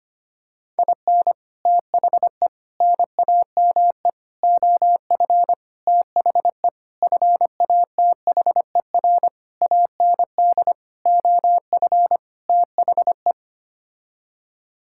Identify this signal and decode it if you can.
Morse code, heard as a keyed tone, about 25 wpm, 700 Hz. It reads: IN THE NAME OF THE FATHER AND OF THE
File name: Sets of 9 Words - 25wpm